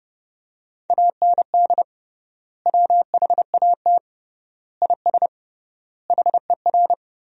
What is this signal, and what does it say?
Morse code, heard as a keyed tone, about 30 wpm, 700 Hz. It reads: AND WHAT IS HER
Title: 3x Word-Spacing - Sentences from Top 200 Words - 30wpm